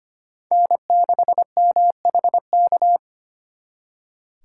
Morse code, heard as a keyed tone, about 32 wpm, 700 Hz. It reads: N6MHK